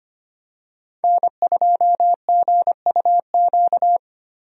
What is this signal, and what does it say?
Morse code, heard as a keyed tone, about 25 wpm, 700 Hz. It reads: N2GUQ
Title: USA Call Signs - 5 Characters - Repeated 2x - 25wpm